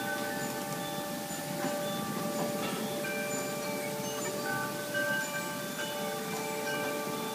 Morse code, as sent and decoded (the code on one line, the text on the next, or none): none